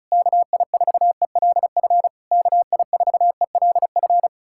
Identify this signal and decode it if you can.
Morse code, heard as a keyed tone, about 40 wpm, 700 Hz. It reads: KI4ELF KI4ELF